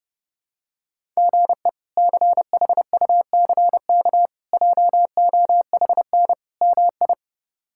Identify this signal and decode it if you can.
Morse code, heard as a keyed tone, about 30 wpm, 700 Hz. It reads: GE CHUCK JOHN MI
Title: K1USN SST QSO Practice - 30wpm